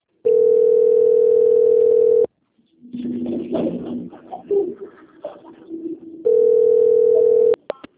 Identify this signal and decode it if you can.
Ringback tone (North America), 440 + 480 Hz.
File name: out- - - - - .63333